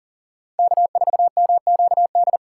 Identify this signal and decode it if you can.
Morse code, heard as a keyed tone, about 52 wpm, 700 Hz. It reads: K4MQD